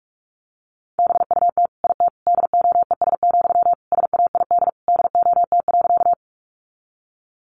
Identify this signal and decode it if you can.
Morse code, heard as a keyed tone, about 44 wpm, 700 Hz. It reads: BUT IT DOES, SAID DOT.
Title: Sentences from Top 1500 Words 45wpm